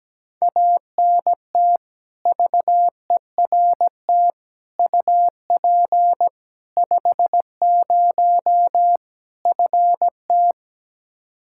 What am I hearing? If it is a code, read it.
Morse code, heard as a keyed tone, about 17 wpm, 700 Hz. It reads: ANT VERT UP 50 FT
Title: Practice QSO - Exchange 2 - Info - 17wpm